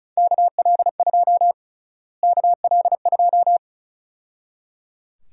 Morse code, heard as a keyed tone, about 34 wpm, 700 Hz. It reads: KL2 KL2
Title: USA Call Sign Prefixes - Repeated 2x - 35wpm